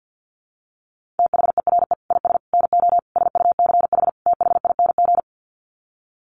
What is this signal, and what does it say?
Morse code, heard as a keyed tone, about 50 wpm, 700 Hz. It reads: THERE IS NO SUCH THING
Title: Sets of 5 Words Encore 50wpm